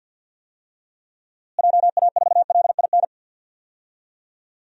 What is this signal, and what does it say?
Morse code, heard as a keyed tone, about 50 wpm, 700 Hz. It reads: JA4LIN
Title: DXCC-100 Call Signs - 6 Characters - Repeated 2x - 50wpm